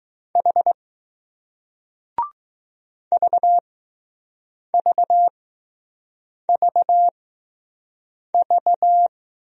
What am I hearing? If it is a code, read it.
Morse code, heard as a keyed tone, about 20 wpm, spaced as for 6 wpm, 700 Hz. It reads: H VVVV